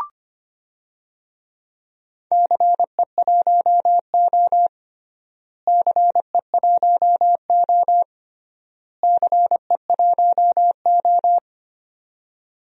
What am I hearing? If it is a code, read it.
Morse code, heard as a keyed tone, about 25 wpm, 700 Hz. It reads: CE1O CE1O CE1O
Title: DXCC-100 Call Signs - 4 Characters - Repeated 3x - 25wpm